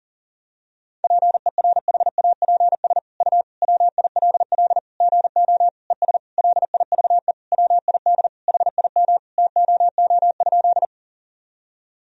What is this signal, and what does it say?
Morse code, heard as a keyed tone, about 37 wpm, 700 Hz. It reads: PERHAPS U WILL GO ES LIVE WID HIM TOO?
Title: Sentences from Top 500 Words and Abbreviations 40wpm